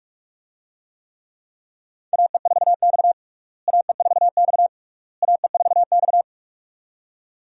Morse code, heard as a keyed tone, about 46 wpm, 700 Hz. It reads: AE4X AE4X AE4X
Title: USA Call Signs - 4 Characters - Repeated 3x - 45wpm